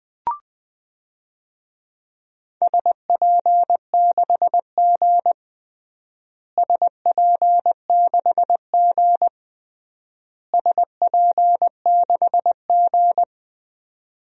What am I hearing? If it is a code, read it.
Morse code, heard as a keyed tone, about 21 wpm, 700 Hz. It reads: SP6G SP6G SP6G
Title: DXCC-100 Call Signs - 4 Characters - Repeated 3x - 20wpm